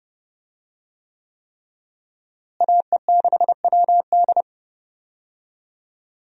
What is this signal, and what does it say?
Morse code, heard as a keyed tone, about 30 wpm, 700 Hz. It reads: AE6WD
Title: USA Call Signs - 5 Characters - Repeated 3x - 30wpm